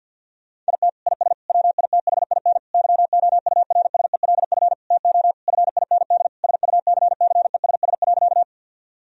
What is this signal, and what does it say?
Morse code, heard as a keyed tone, about 50 wpm, 700 Hz. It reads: IT IS WITHIN YOURSELF TO FIND SUCCESS.
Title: Sentences from Top 1000 Words - Review Part 4 - 50wpm